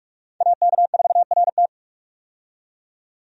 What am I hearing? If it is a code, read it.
Morse code, heard as a keyed tone, about 42 wpm, 700 Hz. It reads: AK4RT
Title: USA Call Signs - 5 Characters - 45wpm